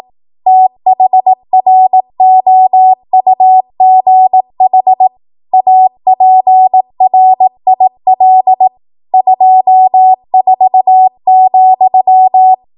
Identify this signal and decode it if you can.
Morse code, heard as a keyed tone, about 18 wpm, 750 Hz. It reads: THROUGH APRIL 24,
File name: Morse / 18WPM